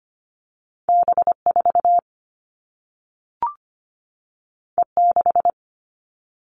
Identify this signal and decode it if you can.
Morse code, heard as a keyed tone, about 25 wpm, 700 Hz. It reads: B4 E6
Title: All Cut Numbers - 2-Digit 25wpm